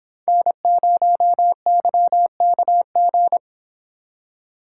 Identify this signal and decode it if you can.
Morse code, heard as a keyed tone, about 26 wpm, 700 Hz. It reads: N0YKG